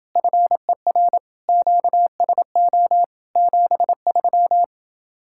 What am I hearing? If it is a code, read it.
Morse code, heard as a keyed tone, about 28 wpm, 700 Hz. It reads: FER QSO 73